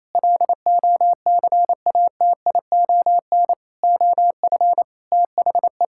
Morse code, heard as a keyed tone, about 28 wpm, 700 Hz. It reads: LOCATION OF THE
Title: Sentences from Top 1000 Words - Review Part 6 - 28wpm